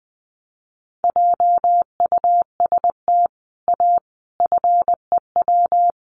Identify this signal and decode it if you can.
Morse code, heard as a keyed tone, about 20 wpm, 700 Hz. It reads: JUST A FEW
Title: Sets of 5 Words Encore 20wpm